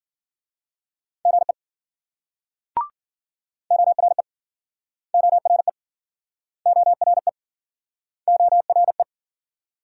Morse code, heard as a keyed tone, about 51 wpm, 700 Hz. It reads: GE ORE ORE ORE ORE